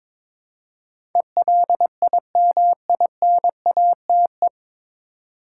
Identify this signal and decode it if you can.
Morse code, heard as a keyed tone, about 22 wpm, 700 Hz. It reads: ELIMINATE
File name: 9 Letter Words - First 3 Letters Given - 22wpm